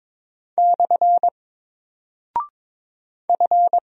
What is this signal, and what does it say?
Morse code, heard as a keyed tone, about 22 wpm, 700 Hz. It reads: / F
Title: Single Character (Letter:Number:Punctuation:Prosign) 22wpm